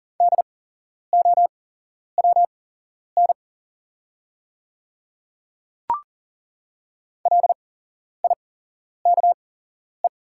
Morse code, heard as a keyed tone, about 40 wpm, spaced as for 11 wpm, 700 Hz. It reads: DOWN LIKE
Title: 8x Character Spacing - Top 200 Words - 40wpm